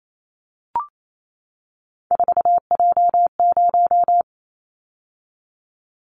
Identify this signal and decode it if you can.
Morse code, heard as a keyed tone, about 28 wpm, 700 Hz. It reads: 4J0
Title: Callsign Prefixes for Top 100 DXCC - 28wpm